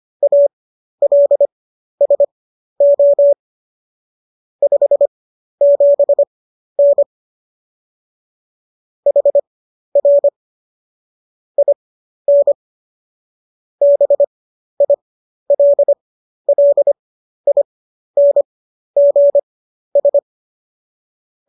Morse code, heard as a keyed tone, about 25 wpm, spaced as for 12 wpm, 570 Hz. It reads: ALSO 57N HR IN BILLINGS